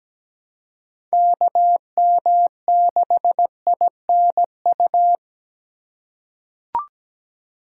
Morse code, heard as a keyed tone, about 17 wpm, 700 Hz. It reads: KM6INU
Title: USA Call Signs - 6 Characters - 2x Character Spacing - 17wpm